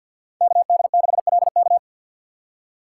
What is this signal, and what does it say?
Morse code, heard as a keyed tone, about 52 wpm, 700 Hz. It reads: KD6LK
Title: USA Call Signs - 5 Characters - 6x Character Spacing - 50wpm